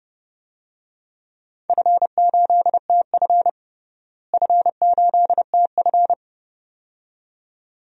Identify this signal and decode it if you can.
Morse code, heard as a keyed tone, about 30 wpm, 700 Hz. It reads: F8TF F8TF
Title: DXCC-100 Call Signs - 4 Characters - Repeated 2x - 30wpm